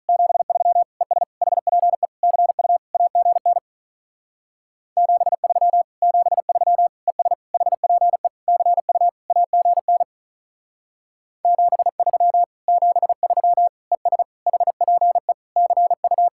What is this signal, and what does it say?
Morse code, heard as a keyed tone, about 44 wpm, 700 Hz. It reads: 73 ES HPE CU AGN 73 73 ES HPE CU AGN 73 73 ES HPE CU
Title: Speed-Racing Common QSO Phrases - 35wpm